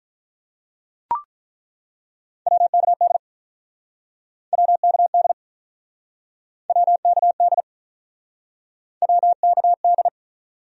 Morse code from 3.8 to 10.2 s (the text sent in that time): WKD WKD WKD